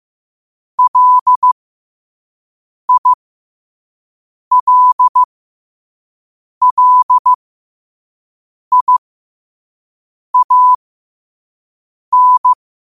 Morse code, heard as a keyed tone, about 15 wpm, spaced as for 5 wpm, 1000 Hz. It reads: LILLIAN
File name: qotdfiles / 5wpm